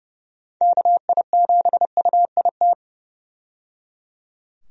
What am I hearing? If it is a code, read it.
Morse code, heard as a keyed tone, about 30 wpm, 700 Hz. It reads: KI7UIT
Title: USA Call Signs - 6 Characters - 30wpm